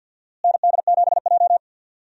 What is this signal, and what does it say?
Morse code, heard as a keyed tone, about 48 wpm, 700 Hz. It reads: ND7J
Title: USA Call Signs - 4 Characters - Repeated 3x - 50wpm